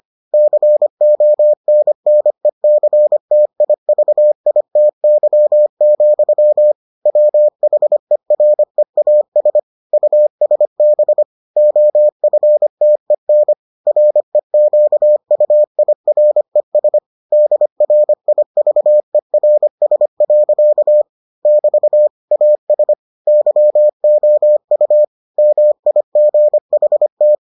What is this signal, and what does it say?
Morse code, heard as a keyed tone, about 25 wpm, 600 Hz. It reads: CONNECTIVITY, WHEREAS USB OFTEN REQUIRES DRIVERS. <BT> AS YOU MIGHT